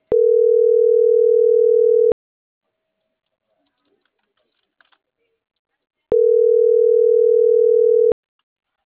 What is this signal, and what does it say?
Ringback tone (North America), 440 + 480 Hz.